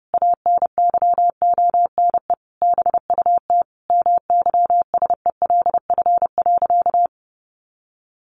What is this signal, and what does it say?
Morse code, heard as a keyed tone, about 30 wpm, 700 Hz. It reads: ANYONE BUT MYSELF.
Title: Sentences from Top 1300 Words 30wpm